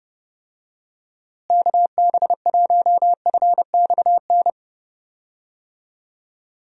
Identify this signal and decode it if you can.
Morse code, heard as a keyed tone, about 32 wpm, 700 Hz. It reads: KB1FXN
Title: USA Call Signs - 6 Characters - Repeated 3x - 30wpm